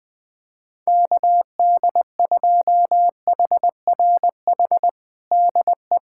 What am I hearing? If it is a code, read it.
Morse code, heard as a keyed tone, about 21 wpm, 700 Hz. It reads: KD2HRH DE
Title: USA Callsign DE Callsign 20wpm